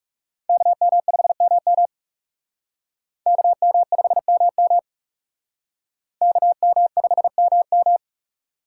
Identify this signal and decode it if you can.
Morse code, heard as a keyed tone, about 36 wpm, 700 Hz. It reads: KM5MM KM5MM KM5MM
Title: Speed Racing - USA Call Signs 30wpm